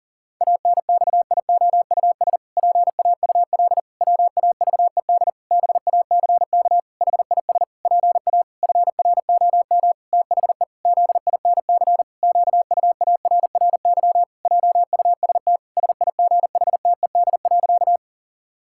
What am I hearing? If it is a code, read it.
Morse code, heard as a keyed tone, about 40 wpm, 700 Hz. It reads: ANXIOUS PAUL WAVED BACK HIS PA FROM THE ZINC QUARRY JUST SIGHTED.
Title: Pangram Challenge v2 - 40wpm